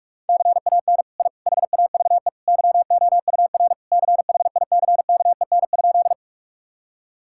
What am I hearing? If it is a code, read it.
Morse code, heard as a keyed tone, about 46 wpm, 700 Hz. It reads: CAN I HAVE YOUR CHICKEN?
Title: Sentences from Top 1000 Words - Review Part 5 - 45wpm